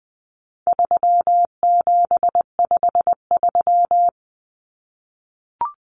Morse code, heard as a keyed tone, about 20 wpm, 700 Hz. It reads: 3753
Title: Common Cut Numbers - 4-Digit 20wpm